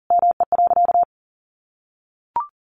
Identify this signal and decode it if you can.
Morse code, heard as a keyed tone, about 40 wpm, 700 Hz. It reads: ME.
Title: Sentences from Top 1400 Words 40wpm